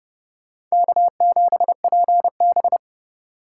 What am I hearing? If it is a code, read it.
Morse code, heard as a keyed tone, about 31 wpm, 700 Hz. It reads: K7PB